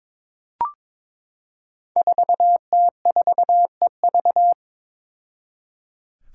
Morse code, heard as a keyed tone, about 22 wpm, 700 Hz. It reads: 4T4EV
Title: Five Challenging Letter-Number Combinations 22wpm